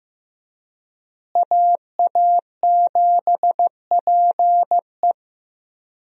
Morse code, heard as a keyed tone, about 15 wpm, 700 Hz. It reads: AA7PE
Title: USA Call Signs - 5 Characters - Repeated 3x - 15wpm